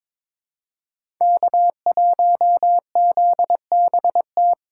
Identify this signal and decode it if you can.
Morse code, heard as a keyed tone, about 21 wpm, 700 Hz. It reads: K1ZBT
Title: USA Call Signs - 5 Characters - 6x Character Spacing - 22wpm